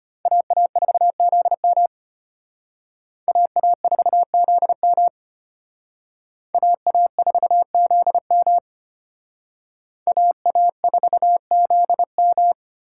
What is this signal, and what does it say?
Morse code, heard as a keyed tone, about 31 wpm, 700 Hz. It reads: AA4ZM AA4ZM AA4ZM AA4ZM
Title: Speed Racing - USA Call Signs 25wpm